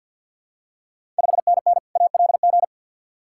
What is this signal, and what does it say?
Morse code, heard as a keyed tone, about 47 wpm, 700 Hz. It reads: 5NN ALG
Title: Ontario QSO Party Exchange 50wpm